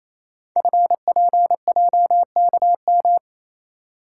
Morse code, heard as a keyed tone, about 28 wpm, 690 Hz. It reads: FPJKM
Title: Code Groups - 5 Letter:Numbers - 28wpm